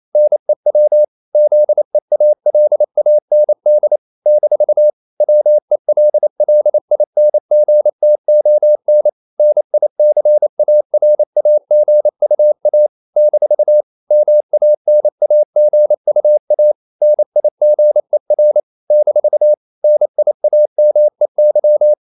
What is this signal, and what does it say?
Morse code, heard as a keyed tone, about 28 wpm, 600 Hz. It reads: NEW ZEALAND - WELLINGTON NICARAGUA - MANAGUA NIGER - NIAMEY
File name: CW / Practice / NationalCapitals28